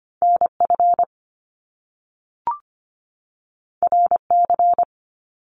Morse code, heard as a keyed tone, about 25 wpm, 700 Hz. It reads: NF RC
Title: Top Two Letter Combinations in Words 25wpm